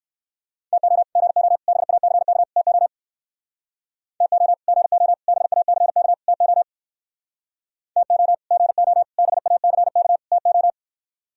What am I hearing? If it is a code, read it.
Morse code, heard as a keyed tone, about 58 wpm, 700 Hz. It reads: TO GO BACK TO TO GO BACK TO TO GO BACK TO